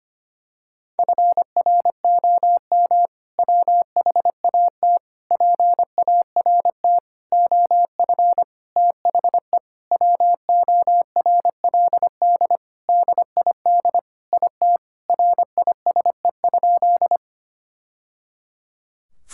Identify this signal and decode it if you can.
Morse code, heard as a keyed tone, about 25 wpm, 700 Hz. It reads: FROM WHAT PART OF THE WORLD DID IT RISE?
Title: Sentences from Top 1000 Words - Review Part 6 - 25wpm